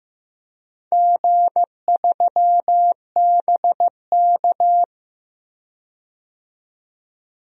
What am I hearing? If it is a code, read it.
Morse code, heard as a keyed tone, about 15 wpm, 700 Hz. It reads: G3BK
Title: DXCC-100 Call Signs - 4 Characters - 15wpm